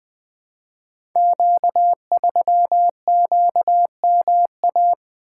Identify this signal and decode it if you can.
Morse code, heard as a keyed tone, about 20 wpm, 700 Hz. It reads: Q3QMA
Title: Code Groups - 5 Letter:Numbers - 20wpm